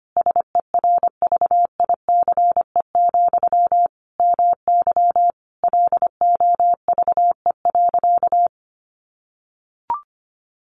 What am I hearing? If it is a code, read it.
Morse code, heard as a keyed tone, about 25 wpm, 700 Hz. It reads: SERVICE, MY LOVE.
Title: Sentences from Top 1100 Words 25wpm